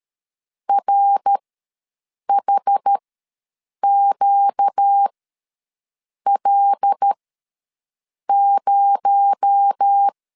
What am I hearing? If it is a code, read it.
Morse code, heard as a keyed tone, about 13 wpm, spaced as for 6 wpm, 800 Hz. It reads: RHQL0
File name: LEZIONI / 7 GF9Z / 7 GF9Z ws 12wpm 3321